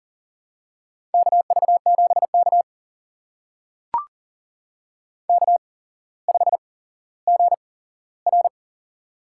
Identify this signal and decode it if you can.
Morse code, heard as a keyed tone, about 42 wpm, spaced as for 17 wpm, 700 Hz. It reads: KV7K K5GR
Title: USA Call Signs - 4 Characters - 8x Character Spacing - 40wpm